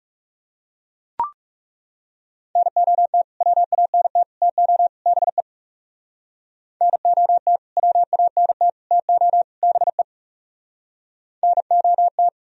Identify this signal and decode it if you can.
Morse code, heard as a keyed tone, about 36 wpm, 700 Hz. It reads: NOT WANT TO BE NOT WANT TO BE NOT
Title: Speed-Racing - Sets of 4 Words 30wpm